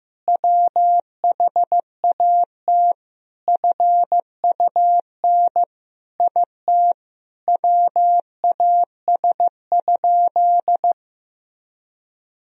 Morse code, heard as a keyed tone, about 16 wpm, 700 Hz. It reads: WHAT FUN IT WAS?
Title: Sentences from Top 1000 Words - Review Part 4 - 15wpm